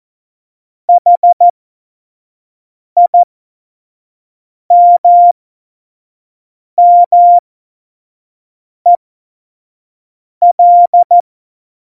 Morse code, heard as a keyed tone, about 14 wpm, spaced as for 5 wpm, 700 Hz. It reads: HIMMEL
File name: Morsekurs / Witze / Witz 29. Dezember 2020, 25 F70